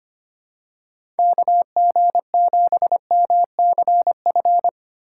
Morse code, heard as a keyed tone, about 25 wpm, 700 Hz. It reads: KG7MCF
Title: USA Call Signs - 6 Characters - 2x Character Spacing - 25wpm